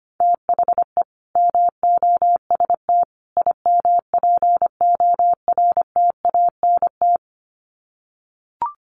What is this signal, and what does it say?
Morse code, heard as a keyed tone, about 25 wpm, 700 Hz. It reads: THE MOST IMPORTANT
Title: Sets of 3 Words 25wpm